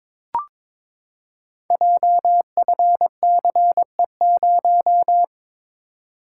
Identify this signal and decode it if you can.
Morse code, heard as a keyed tone, about 22 wpm, 700 Hz. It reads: JFCE0
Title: Code Groups - 5 Letter:Numbers - 22wpm